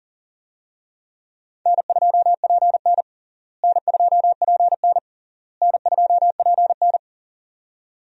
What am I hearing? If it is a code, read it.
Morse code, heard as a keyed tone, about 40 wpm, 700 Hz. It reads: N2PN N2PN N2PN
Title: USA Call Signs - 4 Characters - Repeated 3x - 40wpm